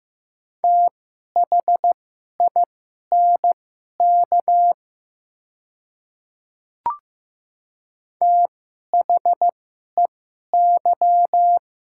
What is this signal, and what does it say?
Morse code, heard as a keyed tone, about 15 wpm, spaced as for 11 wpm, 700 Hz. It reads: THINK THEY
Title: 2x Character Spacing - Top 200 Words - 15wpm